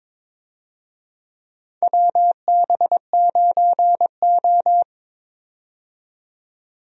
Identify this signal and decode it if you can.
Morse code, heard as a keyed tone, about 21 wpm, 700 Hz. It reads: WB9O
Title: USA Call Signs - 4 Characters - 22wpm